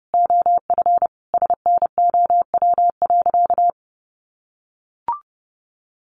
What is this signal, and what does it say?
Morse code, heard as a keyed tone, about 30 wpm, 700 Hz. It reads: OF SNOW.
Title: Easy Phrases 30wpm